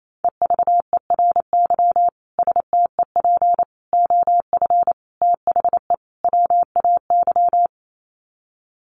Morse code, heard as a keyed tone, about 28 wpm, 700 Hz. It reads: EVERY STEP OF THE WAY
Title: Sets of 5 Words Encore2 28wpm